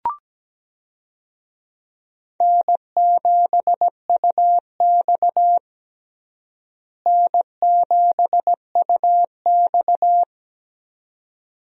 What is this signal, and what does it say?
Morse code, heard as a keyed tone, about 17 wpm, 700 Hz. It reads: N7UX N7UX
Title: USA Call Signs - 4 Characters - Repeated 2x - 17wpm